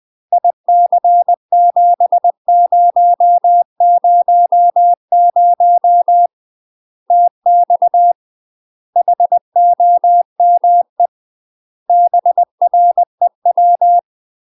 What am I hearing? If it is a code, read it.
Morse code, heard as a keyed tone, about 20 wpm, 700 Hz. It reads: IC7000 TX HOME BREW
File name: UK MASTER QSOs TRAINING / 20WPM / UK MASTER QSOs 0039